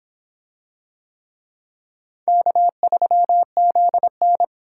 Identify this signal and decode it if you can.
Morse code, heard as a keyed tone, about 23 wpm, 700 Hz. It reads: K3ZN